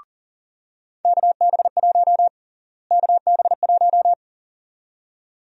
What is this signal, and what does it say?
Morse code, heard as a keyed tone, about 40 wpm, 700 Hz. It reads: KB1 KB1
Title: USA Call Sign Prefixes - Repeated 2x - 40wpm